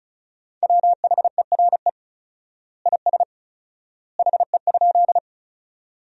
Morse code, heard as a keyed tone, about 34 wpm, 700 Hz. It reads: WHERE IS HE?
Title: 4x Word-Spacing - Sentences from Top 200 Words - 35wpm